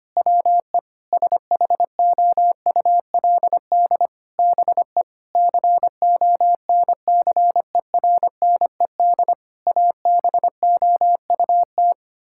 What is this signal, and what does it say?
Morse code, heard as a keyed tone, about 25 wpm, 700 Hz. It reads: WE SHOULD BE CONCERNED ABOUT
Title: Sentences from Top 1000 Words - Review Part 6 - 25wpm